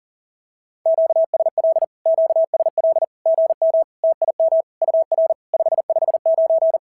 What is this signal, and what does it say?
Morse code, heard as a keyed tone, about 43 wpm, 650 Hz. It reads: QSL QSL GM TIM UR 559
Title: SOTA QSO Practice 40wpm